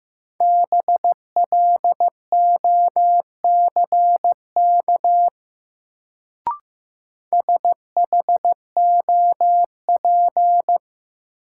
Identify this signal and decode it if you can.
Morse code, heard as a keyed tone, about 15 wpm, 700 Hz. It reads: BLOCK SHOP
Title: Sentences from Top 1600 Words - New Word Drill - 15wpm